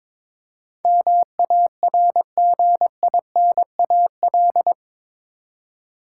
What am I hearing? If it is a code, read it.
Morse code, heard as a keyed tone, about 22 wpm, 700 Hz. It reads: MARGINAL
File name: practice / 22wpm / Eight Letter Words - 22wpm